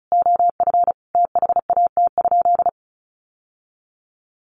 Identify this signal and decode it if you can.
Morse code, heard as a keyed tone, about 34 wpm, 700 Hz. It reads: OF THAT?
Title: Sentences from Top 700 Words 35wpm